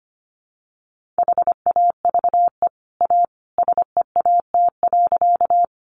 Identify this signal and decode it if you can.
Morse code, heard as a keyed tone, about 25 wpm, 700 Hz. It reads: HAVE A SEAT.
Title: Sentences from Top 1500 Words 25wpm